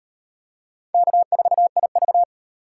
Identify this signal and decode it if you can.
Morse code, heard as a keyed tone, about 38 wpm, 700 Hz. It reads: K4IV